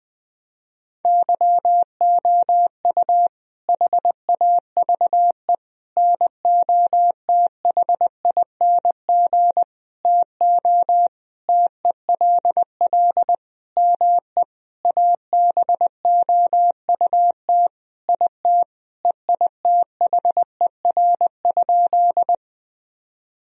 Morse code, heard as a keyed tone, about 20 wpm, 700 Hz. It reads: YOU HAVE NOTHING TO TELL ME ABOUT IT EITHER?
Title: Sentences from Top 1000 Words - Review Part 4 - 20wpm